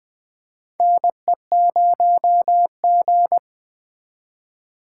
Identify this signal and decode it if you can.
Morse code, heard as a keyed tone, about 20 wpm, 700 Hz. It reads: NE0G
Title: USA Call Signs - 4 Characters - 2x Character Spacing - 20wpm